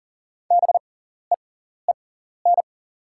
Morse code, heard as a keyed tone, about 40 wpm, spaced as for 14 wpm, 700 Hz. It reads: BEEN
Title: 6x Character Spacing - Top 100 Words - 40wpm